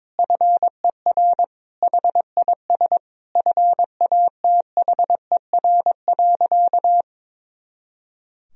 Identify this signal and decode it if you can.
Morse code, heard as a keyed tone, about 22 wpm, 700 Hz. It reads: FER HIS FATHER.
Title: Sentences from Top 500 Words and Abbreviations 22wpm